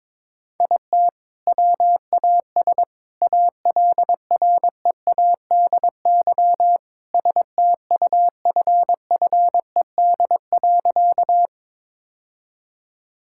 Morse code, heard as a keyed tone, about 22 wpm, 700 Hz. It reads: IT WAS ALREADY STUFFED.
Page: morse-code-ninja-practice-22wpm